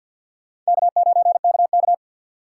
Morse code, heard as a keyed tone, about 62 wpm, 700 Hz. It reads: K9KK
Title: Speed Racing - USA 1x2 Call Signs 50wpm